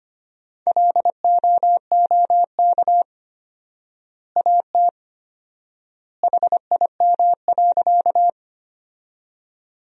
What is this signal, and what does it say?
Morse code, heard as a keyed tone, about 25 wpm, 700 Hz. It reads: LOOK AT HIM.
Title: 4x Word-Spacing - Sentences from Top 100 Words - 25wpm